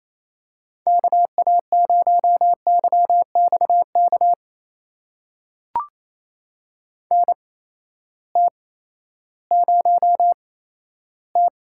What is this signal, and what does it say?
Morse code, heard as a keyed tone, about 28 wpm, spaced as for 14 wpm, 700 Hz. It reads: KA0YXK NT0T
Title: USA Call Signs - 8x Character Spacing - 28wpm